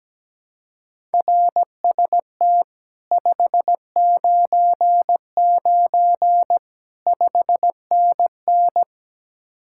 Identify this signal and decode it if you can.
Morse code, heard as a keyed tone, about 17 wpm, 700 Hz. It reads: RST 599 5NN
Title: Speed-Racing Common QSO Phrases Encore - 17wpm